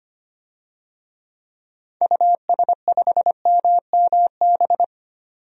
Morse code, heard as a keyed tone, about 25 wpm, 700 Hz. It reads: US5MMB
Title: DXCC-100 Call Signs - 6 Characters - 25wpm